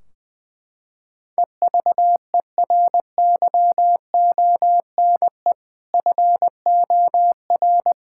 Morse code, heard as a keyed tone, about 21 wpm, 700 Hz. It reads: EVERYONE FOR
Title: Sentences from Top 1600 Words 20wpm